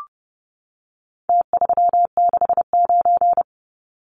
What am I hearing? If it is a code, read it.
Morse code, heard as a keyed tone, about 30 wpm, 700 Hz. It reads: T369
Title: Numbers - Contest Style - 4-Digit 30wpm